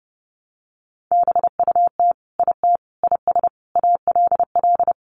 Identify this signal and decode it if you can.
Morse code, heard as a keyed tone, about 30 wpm, 700 Hz. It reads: BUT IT IS ALL
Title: Sentences from Top 300 Words 30wpm